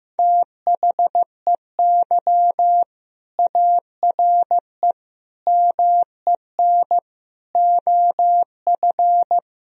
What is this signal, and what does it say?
Morse code, heard as a keyed tone, about 15 wpm, 700 Hz. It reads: THEY ARE MEN OF
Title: Sentences from Top 1000 Words - Review Part 2 - 15wpm